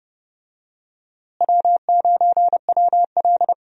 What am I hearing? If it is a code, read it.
Morse code, heard as a keyed tone, about 32 wpm, 700 Hz. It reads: W9WL
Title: USA Call Signs - 4 Characters - Repeated 3x - 30wpm